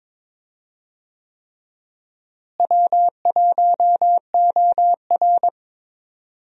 Morse code, heard as a keyed tone, about 22 wpm, 700 Hz. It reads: W1OR
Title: USA Call Signs - 4 Characters - Repeated 3x - 22wpm